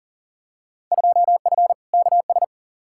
Morse code, heard as a keyed tone, about 40 wpm, 700 Hz. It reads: 2F KS
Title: Field Day Exchanges Encore - 40wpm